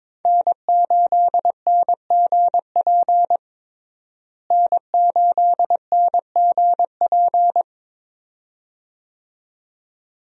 Morse code, heard as a keyed tone, about 22 wpm, 700 Hz. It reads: N8NGP N8NGP
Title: USA Call Signs - 5 Characters - Repeated 2x - 22wpm